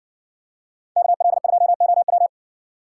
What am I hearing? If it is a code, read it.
Morse code, heard as a keyed tone, about 60 wpm, 700 Hz. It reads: KL2OW